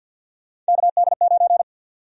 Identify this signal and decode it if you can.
Morse code, heard as a keyed tone, about 53 wpm, 700 Hz. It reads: KD9
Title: USA Call Sign Prefixes - 50wpm